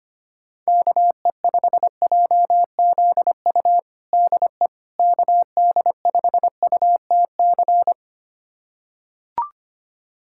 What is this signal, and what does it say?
Morse code, heard as a keyed tone, about 25 wpm, 700 Hz. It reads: KE5JZU DE KD5UTC
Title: USA Callsign DE Callsign 25wpm